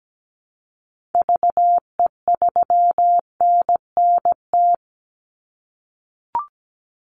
Morse code, heard as a keyed tone, about 17 wpm, 700 Hz. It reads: VE3NNT
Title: Call Signs - 17wpm